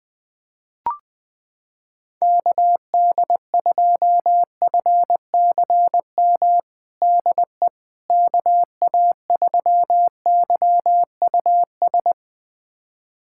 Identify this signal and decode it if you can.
Morse code, heard as a keyed tone, about 20 wpm, 700 Hz. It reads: KD2FCM DE KA3YUS
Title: USA Callsign DE Callsign - 6 Characters - 20wpm